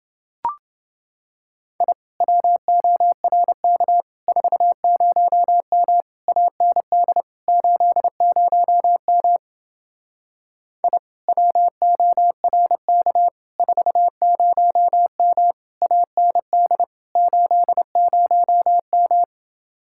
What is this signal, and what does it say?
Morse code, heard as a keyed tone, about 29 wpm, 700 Hz. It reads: I WORK 40M AND 80M I WORK 40M AND 80M